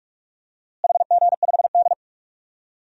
Morse code, heard as a keyed tone, about 46 wpm, 700 Hz. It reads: HG5D
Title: DXCC-100 Call Signs - 4 Characters - Repeated 2x - 45wpm